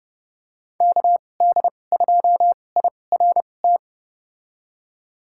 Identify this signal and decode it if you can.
Morse code, heard as a keyed tone, about 29 wpm, spaced as for 21 wpm, 700 Hz. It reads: KD2IRT
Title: USA Call Signs - 2x Character Spacing - 30wpm